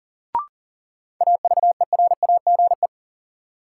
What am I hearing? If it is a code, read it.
Morse code, heard as a keyed tone, about 40 wpm, 700 Hz. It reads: AVERAGE
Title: Sets of 5 Words - New Word Drill - 40wpm